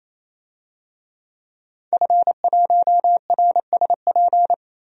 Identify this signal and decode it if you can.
Morse code, heard as a keyed tone, about 28 wpm, 700 Hz. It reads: F1RSP
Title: DXCC-100 Call Signs - 5 Characters - 28wpm